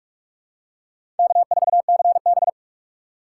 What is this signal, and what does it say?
Morse code, heard as a keyed tone, about 43 wpm, 700 Hz. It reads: K4CB
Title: USA Call Signs - 4 Characters - 6x Character Spacing - 45wpm